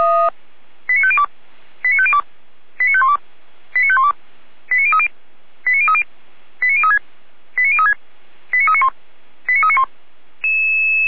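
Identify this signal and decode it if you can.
Five-tone selective call (ZVEI), heard as quick runs of five tones, tones of about 70 ms. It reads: 89502 89502 89512 89512 89030 89030 89036 89036 89391 89391
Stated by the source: ZVEI five-tone call received on an SDR.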